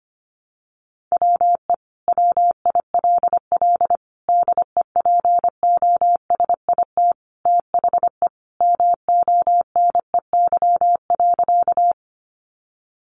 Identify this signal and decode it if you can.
Morse code, heard as a keyed tone, about 25 wpm, 700 Hz. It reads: WE WILL DEPOSIT THE MONEY.
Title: Sentences from Top 1300 Words 25wpm